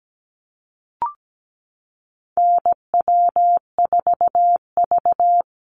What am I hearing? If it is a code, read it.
Morse code, heard as a keyed tone, about 17 wpm, 700 Hz. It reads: NW4V
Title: Call Signs Repeated 3x - 17wpm